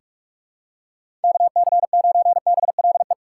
Morse code, heard as a keyed tone, about 43 wpm, 700 Hz. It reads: KC9BLE
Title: USA Call Signs - 6 Characters - 6x Character Spacing - 45wpm